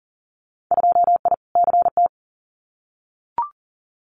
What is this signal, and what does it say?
Morse code, heard as a keyed tone, about 38 wpm, 700 Hz. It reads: 2I CT
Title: Winter Field Day Exchanges 40wpm